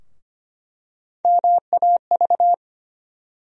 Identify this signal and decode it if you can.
Morse code, heard as a keyed tone, about 25 wpm, 700 Hz. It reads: MAV
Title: USA Call Sign Postfixes - 3 Letters 25wpm